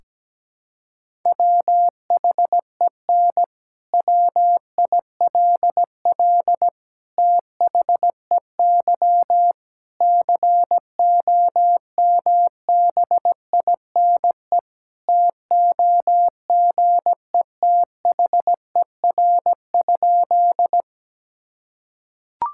Morse code, heard as a keyed tone, about 17 wpm, 700 Hz. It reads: WHEN WILL THEY COMBINE TOGETHER?
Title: Sentences from Top 1600 Words 17wpm